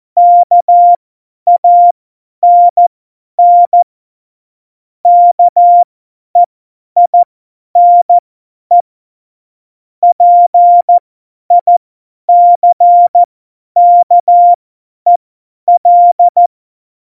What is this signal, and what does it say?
Morse code, heard as a keyed tone, about 14 wpm, spaced as for 10 wpm, 700 Hz. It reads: KANN KEINE PICKEL
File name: Morsekurs / Witze / Witz 21. November 2017, 50 F70